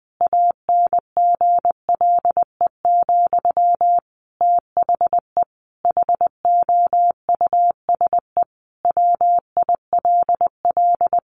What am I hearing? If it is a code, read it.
Morse code, heard as a keyed tone, about 20 wpm, 700 Hz. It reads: ANGLE, THE HOUSE WILL
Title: Sentences from Top 1500 Words 20wpm